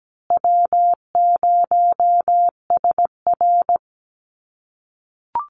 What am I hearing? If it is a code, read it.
Morse code, heard as a keyed tone, about 17 wpm, 700 Hz. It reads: W0SR
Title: USA Call Signs Repeated 3x 17wpm